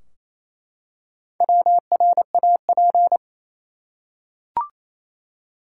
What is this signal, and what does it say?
Morse code, heard as a keyed tone, about 28 wpm, 700 Hz. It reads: WRAP
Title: Four Letter Words - 28wpm